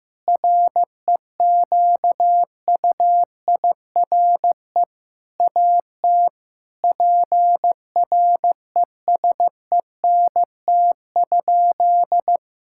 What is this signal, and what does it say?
Morse code, heard as a keyed tone, about 15 wpm, 700 Hz. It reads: REQUIRE AT PRESENT?
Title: Sentences from Top 1000 Words - Review Part 6 - 15wpm